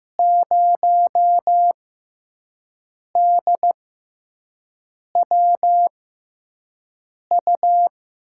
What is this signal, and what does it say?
Morse code, heard as a keyed tone, about 15 wpm, spaced as for 5 wpm, 700 Hz. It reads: 0DWU
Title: USA Call Signs - 6 Characters - 6x Character Spacing - 15wpm